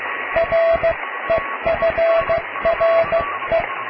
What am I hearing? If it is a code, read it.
Morse code, heard as a keyed tone, about 16 wpm, spaced as for 14 wpm, 660 Hz, with radio noise. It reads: REFRE
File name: cw / tekst(a)+QRM